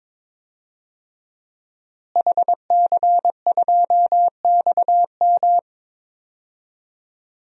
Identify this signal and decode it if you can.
Morse code, heard as a keyed tone, about 22 wpm, 700 Hz. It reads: HC2XM
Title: DXCC-100 Call Signs - 5 Characters - Repeated 2x - 22wpm